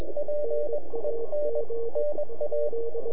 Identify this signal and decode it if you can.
Morse code, heard as a keyed tone, about 21 wpm, spaced as for 18 wpm, 590 Hz, with radio noise. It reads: PENNA
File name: NAQPCW-8-2009 / W1UJ NAQP CW 09 2230Z